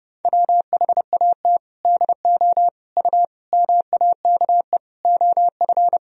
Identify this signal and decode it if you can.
Morse code, heard as a keyed tone, about 32 wpm, 700 Hz. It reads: WHAT DO U MAKE OF
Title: Sentences from Top 100 Words and Abbreviations 30wpm